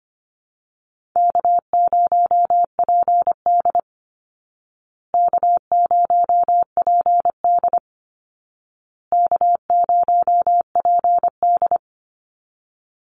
Morse code, heard as a keyed tone, about 25 wpm, 700 Hz. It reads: K0PD K0PD K0PD
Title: Easy Call Signs Repeated 3x - 25wpm